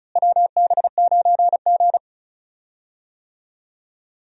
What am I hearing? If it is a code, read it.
Morse code, heard as a keyed tone, about 36 wpm, 700 Hz. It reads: WB9G
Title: USA Call Signs - 4 Characters - Repeated 2x - 35wpm